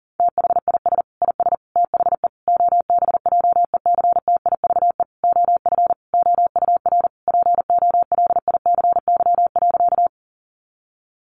Morse code, heard as a keyed tone, about 37 wpm, 700 Hz. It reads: THIS IS THE OBJECTIVE OF OUR POLICY.
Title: Sentences from Top 1200 Words 40wpm